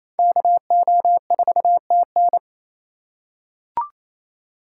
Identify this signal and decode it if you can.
Morse code, heard as a keyed tone, about 28 wpm, 700 Hz. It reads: KO4TN
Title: USA Call Signs - 5 Characters - 2x Character Spacing - 28wpm